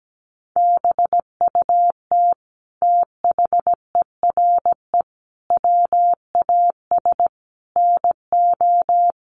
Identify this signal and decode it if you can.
Morse code, heard as a keyed tone, about 17 wpm, 700 Hz. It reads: BUT THERE WAS NO
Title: Sentences from Top 1500 Words 17wpm